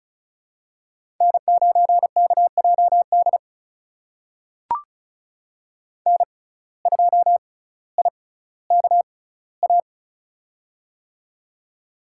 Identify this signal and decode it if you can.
Morse code, heard as a keyed tone, about 36 wpm, spaced as for 18 wpm, 700 Hz. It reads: N9KJD N2IKA
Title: USA Call Signs - 6x Character Spacing - 35wpm